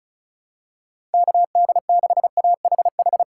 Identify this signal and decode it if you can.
Morse code, heard as a keyed tone, about 34 wpm, 700 Hz. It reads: KD6AHH
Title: USA Call Signs - 3x Character Spacing - 35wpm